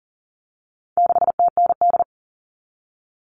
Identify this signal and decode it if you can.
Morse code, heard as a keyed tone, about 40 wpm, 700 Hz. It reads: 6TND
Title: All Cut Numbers - 4-Digit 40wpm